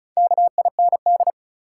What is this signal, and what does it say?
Morse code, heard as a keyed tone, about 36 wpm, 700 Hz. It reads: KIND
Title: Sets of 2 Words - New Word Drill - 35wpm